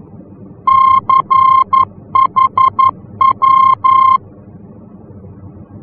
Morse code, heard as a keyed tone, about 11 wpm, 1060 Hz. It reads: CHW
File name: france / mx / CHW-F-VOR